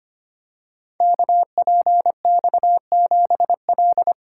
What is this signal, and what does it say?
Morse code, heard as a keyed tone, about 25 wpm, 700 Hz. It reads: KPX7L
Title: Code Groups - 5 Letter:Numbers - 25wpm